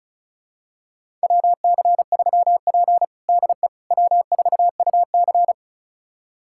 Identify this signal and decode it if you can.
Morse code, heard as a keyed tone, about 34 wpm, 700 Hz. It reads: WC3P DE W4UC
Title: USA Callsign DE Callsign - 4 Characters - 35wpm